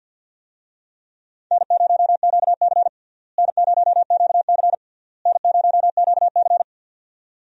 Morse code, heard as a keyed tone, about 49 wpm, 700 Hz. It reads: N0QC N0QC N0QC
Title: USA Call Signs - 4 Characters - Repeated 3x - 50wpm